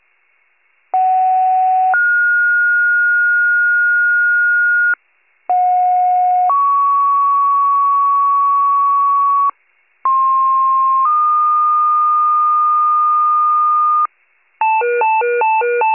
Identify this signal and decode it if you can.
Two-tone page, heard as a one-second tone, then a three-second tone, about 750 then 1430 Hz, about 730 then 1120 Hz, about 1060 then 1220 Hz. It is heard from 0.9 to 14.1 s.
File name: wcalert